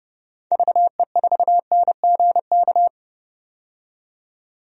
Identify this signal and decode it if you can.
Morse code, heard as a keyed tone, about 32 wpm, 700 Hz. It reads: VE4NGK
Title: DXCC-100 Call Signs - 6 Characters - 30wpm